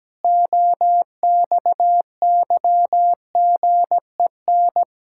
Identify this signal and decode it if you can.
Morse code, heard as a keyed tone, about 17 wpm, 700 Hz. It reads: OXYGEN
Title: Pangram Challenge v2 - 17wpm